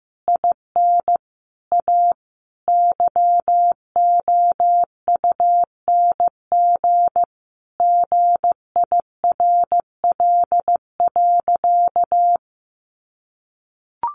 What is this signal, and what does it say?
Morse code, heard as a keyed tone, about 15 wpm, 700 Hz. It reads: IN A YOUNG GIRL.
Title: Sentences from Top 1400 Words 15wpm